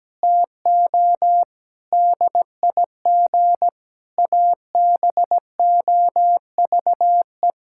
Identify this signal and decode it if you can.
Morse code, heard as a keyed tone, about 17 wpm, 700 Hz. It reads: TO DIG ABOVE
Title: Sentences from Top 1600 Words 17wpm